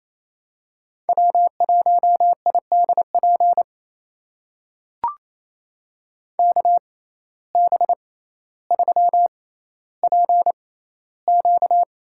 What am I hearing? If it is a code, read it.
Morse code, heard as a keyed tone, about 28 wpm, spaced as for 14 wpm, 700 Hz. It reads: W1IDP KB3PQ
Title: USA Call Signs - 6x Character Spacing - 28wpm